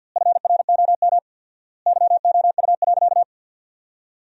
Morse code, heard as a keyed tone, about 50 wpm, 700 Hz. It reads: FROM YOU.
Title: 4x Word-Spacing - Sentences from Top 100 Words - 50wpm